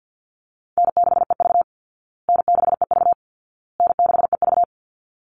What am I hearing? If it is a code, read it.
Morse code, heard as a keyed tone, about 50 wpm, 700 Hz. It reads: N6EV N6EV N6EV
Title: Call Signs Repeated 2x - 50wpm